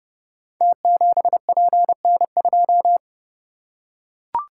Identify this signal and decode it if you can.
Morse code, heard as a keyed tone, about 29 wpm, 700 Hz. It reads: T7PN2
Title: Code Groups - 5 Letter:Numbers - 30wpm